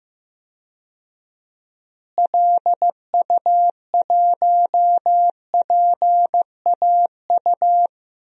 Morse code, heard as a keyed tone, about 15 wpm, 700 Hz. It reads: LU1PAU
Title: DXCC-100 Call Signs - 6 Characters - Repeated 2x - 15wpm